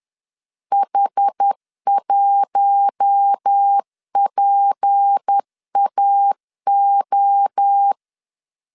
Morse code, heard as a keyed tone, about 10 wpm, 800 Hz. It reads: H1PAO
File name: CW / DET / 1 6 1500C S B 10wpm 4137